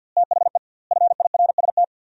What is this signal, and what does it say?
Morse code, heard as a keyed tone, about 52 wpm, 700 Hz. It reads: THE FIRST
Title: 2x Word-Spacing - Sentences from Top 200 Words - 50wpm